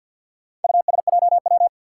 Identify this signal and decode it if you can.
Morse code, heard as a keyed tone, about 52 wpm, 700 Hz. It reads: USJW
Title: Four Challenging Letter-Number Combinations 50wpm